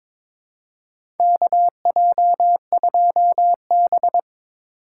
Morse code, heard as a keyed tone, about 22 wpm, 700 Hz. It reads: KJ2B
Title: USA Call Signs - 4 Characters - 8x Character Spacing - 22wpm